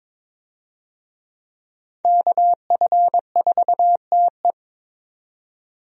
Morse code, heard as a keyed tone, about 22 wpm, 700 Hz. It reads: KF4TE
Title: USA Call Signs - 5 Characters - 22wpm